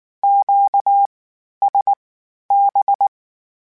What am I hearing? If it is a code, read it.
Morse code, heard as a keyed tone, about 19 wpm, spaced as for 11 wpm, 800 Hz. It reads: QSB
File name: (spaced) / Q-Calls Spaced-mp3s / Q Calls Twice sp @18